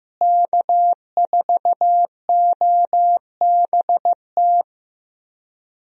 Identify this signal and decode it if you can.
Morse code, heard as a keyed tone, about 15 wpm, 700 Hz. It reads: K4OBT
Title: USA Call Signs - 5 Characters - Repeated 3x - 15wpm